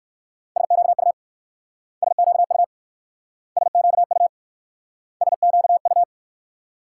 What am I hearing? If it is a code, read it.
Morse code, heard as a keyed tone, about 59 wpm, 700 Hz. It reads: SQU SQU SQU SQU